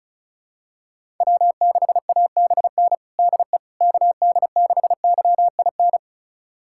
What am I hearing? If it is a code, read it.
Morse code, heard as a keyed tone, about 34 wpm, 700 Hz. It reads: W6ABN DE KD6YIN
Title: USA Callsign DE Callsign 35wpm